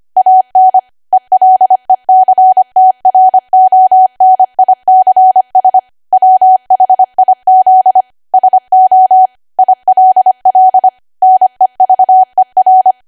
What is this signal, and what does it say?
Morse code, heard as a keyed tone, about 25 wpm, 750 Hz. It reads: AN ELECTRONICS WHIZ SO ILL NEVER
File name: Archive / 25 WPM / 25WPM